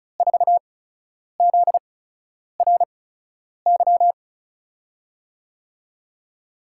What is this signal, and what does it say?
Morse code, heard as a keyed tone, about 36 wpm, spaced as for 10 wpm, 700 Hz. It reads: 4ZRY
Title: USA Call Signs - 8x Character Spacing - 35wpm